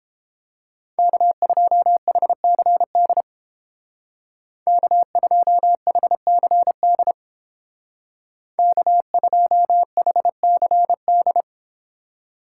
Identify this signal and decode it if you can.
Morse code, heard as a keyed tone, about 30 wpm, 700 Hz. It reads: K2HCD K2HCD K2HCD